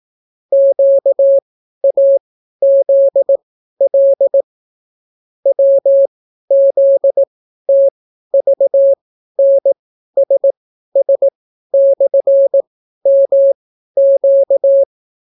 Morse code, heard as a keyed tone, about 18 wpm, spaced as for 12 wpm, 550 Hz. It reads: QAZL WZTVNSS/MQ